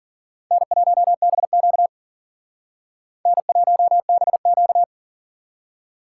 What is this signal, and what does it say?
Morse code, heard as a keyed tone, about 51 wpm, 700 Hz. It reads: N1BQ N1BQ